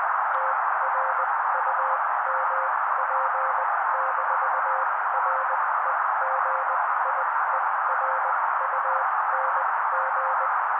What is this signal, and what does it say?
Morse code, heard as a keyed tone, about 20 wpm, spaced as for 16 wpm, 530 Hz. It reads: TRUMP-REGIERUNG
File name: daily / 16wpm noise